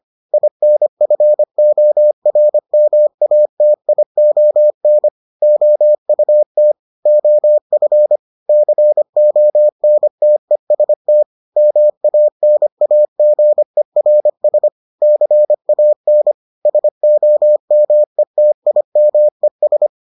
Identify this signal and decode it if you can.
Morse code, heard as a keyed tone, about 25 wpm, 600 Hz. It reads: INFORMATION OUT OF CONTEST MANAGERS CAN SOMETIMES